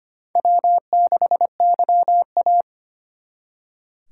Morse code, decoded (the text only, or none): W6YA